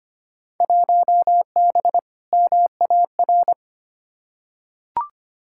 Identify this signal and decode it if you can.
Morse code, heard as a keyed tone, about 25 wpm, 700 Hz. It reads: 1B MAR
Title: Field Day Exchanges Encore - 25wpm